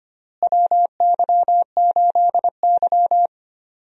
Morse code, heard as a keyed tone, about 25 wpm, 700 Hz. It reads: WY8Y
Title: USA Call Signs - 4 Characters - Repeated 2x - 25wpm